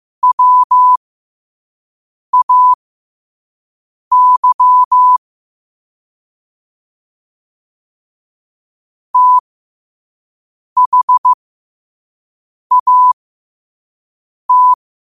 Morse code, heard as a keyed tone, about 15 wpm, spaced as for 5 wpm, 1000 Hz. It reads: WAY THAT
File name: qotdfiles / 5wpm